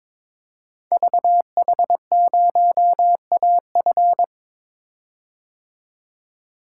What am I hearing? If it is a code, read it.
Morse code, heard as a keyed tone, about 22 wpm, 700 Hz. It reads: VH0AF
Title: Code Groups - 5 Letter:Numbers - 22wpm